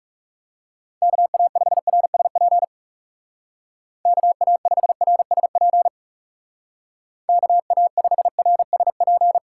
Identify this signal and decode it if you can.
Morse code, heard as a keyed tone, about 37 wpm, 700 Hz. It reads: KA5RSP KA5RSP KA5RSP
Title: Speed Racing - USA Call Signs 30wpm